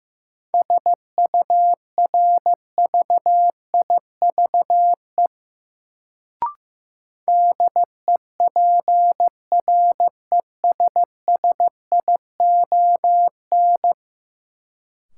Morse code, heard as a keyed tone, about 15 wpm, 700 Hz. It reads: SURVIVE DEPRESSION
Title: Sentences from Top 1300 Words - New Word Drill - 15wpm